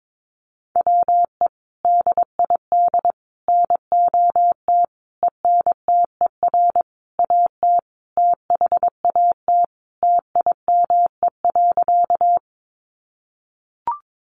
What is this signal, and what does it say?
Morse code, heard as a keyed tone, about 22 wpm, 700 Hz. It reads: WE DID NOT ENTER AT THAT TIME.
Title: Sentences from Top 1300 Words 22wpm